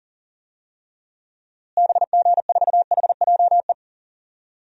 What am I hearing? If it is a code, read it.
Morse code, heard as a keyed tone, about 37 wpm, 700 Hz. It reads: BG4HJE
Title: DXCC-100 Call Signs - 6 Characters - Repeated 3x - 40wpm